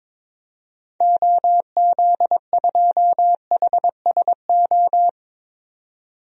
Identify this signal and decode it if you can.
Morse code, heard as a keyed tone, about 22 wpm, 700 Hz. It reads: OZ2HSO
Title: DXCC-100 Call Signs - 6 Characters - 22wpm